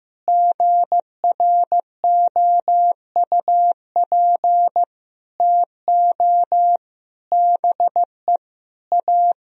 Morse code, heard as a keyed tone, about 15 wpm, 700 Hz. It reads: GROUP TO BE A
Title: Sentences from Top 1000 Words - Review Part 1 - 15wpm